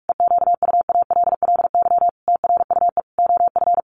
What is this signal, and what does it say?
Morse code, heard as a keyed tone, about 44 wpm, 700 Hz. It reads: EQUALLY TRUE OF
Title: Sentences from Top 1400 Words 45wpm